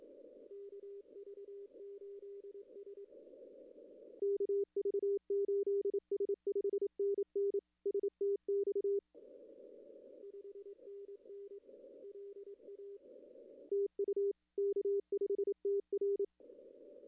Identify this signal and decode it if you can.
Morse code, heard as a keyed tone, about 26 wpm, 410 Hz. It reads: KV8S KV8S 5NN STX 5NN LA TU K5TR
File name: contests / arrl-160-k5tr-op-w5tn -2300